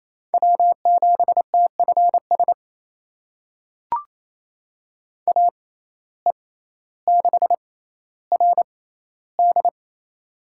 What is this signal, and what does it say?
Morse code, heard as a keyed tone, about 28 wpm, spaced as for 14 wpm, 700 Hz. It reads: W7TFS AE6RD
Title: USA Call Signs - 5 Characters - 6x Character Spacing - 28wpm